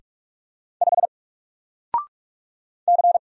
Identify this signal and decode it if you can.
Morse code, heard as a keyed tone, about 46 wpm, 700 Hz. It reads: 5 C
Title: Single Letter-Number Alt 45wpm